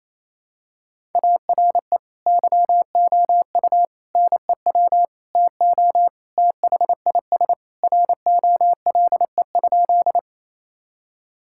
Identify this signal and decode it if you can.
Morse code, heard as a keyed tone, about 28 wpm, 700 Hz. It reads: ARE YOU NEW TO THIS ROLE?
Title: Sentences from Top 1000 Words - Review Part 6 - 28wpm